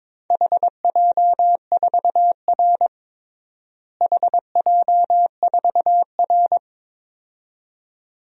Morse code, heard as a keyed tone, about 22 wpm, 700 Hz. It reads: HJ4R HJ4R
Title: DXCC-100 Call Signs - 4 Characters - Repeated 2x - 22wpm